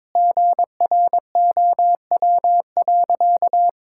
Morse code, heard as a keyed tone, about 22 wpm, 700 Hz. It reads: GROW.
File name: Sentences with 9-Word Phrases - 22wpm